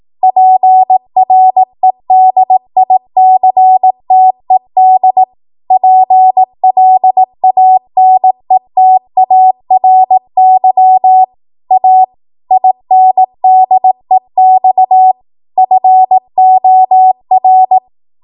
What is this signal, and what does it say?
Morse code, heard as a keyed tone, about 18 wpm, 750 Hz. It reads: PREDICTED PLANETARY A INDEX FOR
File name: Morse / 18WPM